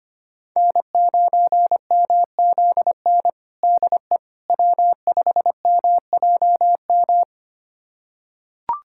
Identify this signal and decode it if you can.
Morse code, heard as a keyed tone, about 25 wpm, 700 Hz. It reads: N9MZN DE W5MJM
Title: USA Callsign DE Callsign - 5 Characters - 25wpm